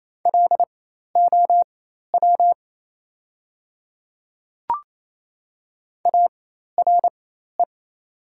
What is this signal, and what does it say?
Morse code, heard as a keyed tone, about 27 wpm, spaced as for 13 wpm, 700 Hz. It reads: LOW ARE
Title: 4x Character Spacing - Top 500 Words - 28wpm